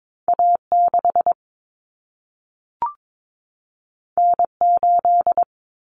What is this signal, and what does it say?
Morse code, heard as a keyed tone, about 22 wpm, 700 Hz. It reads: A6 N8
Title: Common Cut Numbers - 2-Digit 22wpm